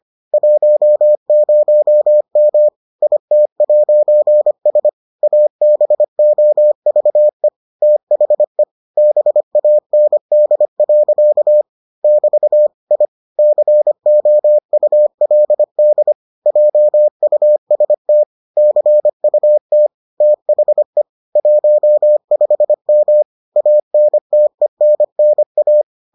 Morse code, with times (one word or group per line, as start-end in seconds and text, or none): 0.3-2.7 10M
3.0-4.9 IT'S
5.2-7.5 ABOVE
7.8-8.6 THE
9.0-11.6 BAND.
12.0-12.7 <BT>
12.9-13.1 I
13.4-16.1 COULD
16.5-18.2 JUST
18.6-19.9 CUT
20.2-21.0 THE
21.4-23.2 15M
23.6-25.8 ANTENNA